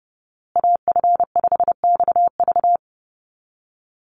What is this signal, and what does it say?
Morse code, heard as a keyed tone, about 30 wpm, 700 Hz. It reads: AF5XV
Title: USA Call Signs Repeated 2x 30wpm